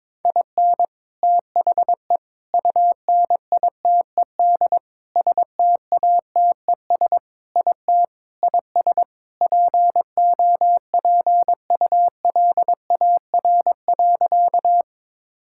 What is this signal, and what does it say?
Morse code, heard as a keyed tone, about 22 wpm, 700 Hz. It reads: IN THE UNITED STATES IT IS POPULAR.
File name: practice / 22wpm / Sentences with 3-Word Phrases - 22wpm